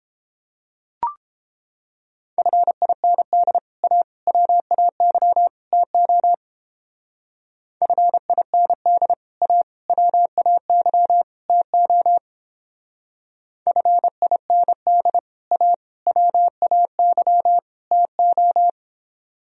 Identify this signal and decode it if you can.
Morse code, heard as a keyed tone, about 29 wpm, 700 Hz. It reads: FIND A WAY TO FIND A WAY TO FIND A WAY TO